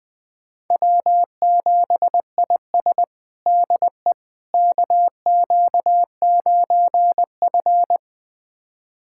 Morse code, heard as a keyed tone, about 20 wpm, 700 Hz. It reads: W7IS DE KQ9F
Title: USA Callsign DE Callsign - 4 Characters - 20wpm